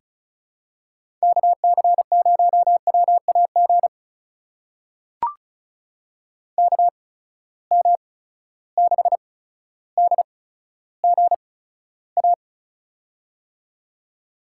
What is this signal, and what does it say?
Morse code, heard as a keyed tone, about 36 wpm, spaced as for 15 wpm, 700 Hz. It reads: KC0WAG KM6DGA
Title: USA Call Signs - 6 Characters - 8x Character Spacing - 35wpm